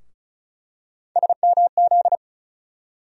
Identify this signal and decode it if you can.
Morse code, heard as a keyed tone, about 36 wpm, 700 Hz. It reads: SMZ
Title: USA Call Sign Postfixes - 3 Letters 35wpm